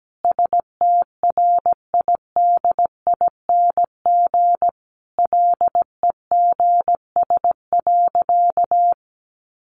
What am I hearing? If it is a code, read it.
Morse code, heard as a keyed tone, about 17 wpm, 700 Hz. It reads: STRIDING LEGS.
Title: A Sound Of Thunder 17wpm